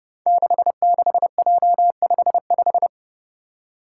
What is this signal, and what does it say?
Morse code, heard as a keyed tone, about 30 wpm, 700 Hz. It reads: 66J55
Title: Five Challenging Letter-Number Combinations 30wpm